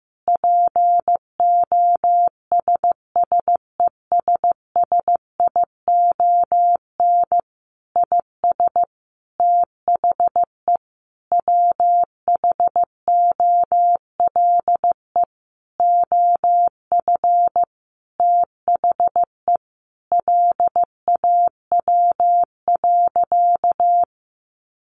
Morse code, heard as a keyed tone, about 15 wpm, 700 Hz. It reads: POSSESSION IS THE WHOLE OF THE LAW.
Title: Sentences from Top 1400 Words 15wpm